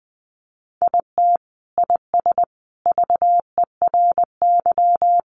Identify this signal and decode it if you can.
Morse code, heard as a keyed tone, about 20 wpm, 700 Hz. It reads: IT IS VERY
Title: Sentences from Top 200 Words 20wpm